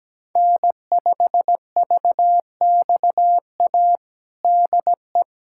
Morse code, heard as a keyed tone, about 17 wpm, 700 Hz. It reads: N5VXA DE
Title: USA Callsign DE Callsign - 5 Characters - 17wpm